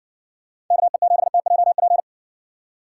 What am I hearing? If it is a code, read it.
Morse code, heard as a keyed tone, about 60 wpm, 700 Hz. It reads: KE7TJP